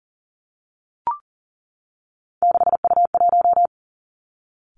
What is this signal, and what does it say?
Morse code, heard as a keyed tone, about 38 wpm, 700 Hz. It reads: 6U1
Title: Three Challenging Letter-Number Combinations 40wpm